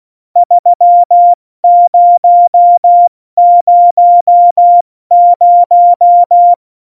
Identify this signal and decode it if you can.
Morse code, heard as a keyed tone, about 16 wpm, spaced as for 14 wpm, 700 Hz. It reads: 3000
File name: Morsekurs / Witze / Witz 04. Dezember, 70 F80